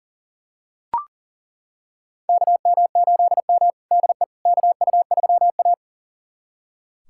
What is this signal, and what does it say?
Morse code, heard as a keyed tone, about 40 wpm, 700 Hz. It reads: KM8M DE KU3A
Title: USA Callsign DE Callsign - 4 Characters - 40wpm